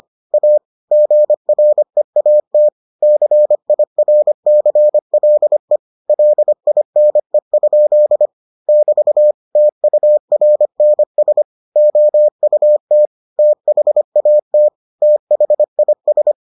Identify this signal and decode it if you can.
Morse code, heard as a keyed tone, about 25 wpm, 600 Hz. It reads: A GREAT CIRCLE LINE? <BT> TURNS OUT THAT THIS